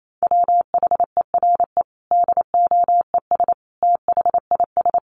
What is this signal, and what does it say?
Morse code, heard as a keyed tone, about 28 wpm, 700 Hz. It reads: WHERE DOES THIS
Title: Sentences from Top 1400 Words 28wpm